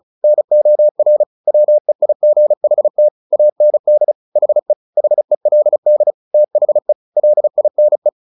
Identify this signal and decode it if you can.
Morse code, heard as a keyed tone, about 34 wpm, 600 Hz. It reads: NOR WEIGHT AND HE HELD THE LINE
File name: the old man and the sea / the old man and the sea 1 35wpm